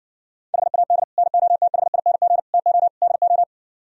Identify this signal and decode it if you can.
Morse code, heard as a keyed tone, about 61 wpm, 700 Hz. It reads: HAD NOTHING TO DO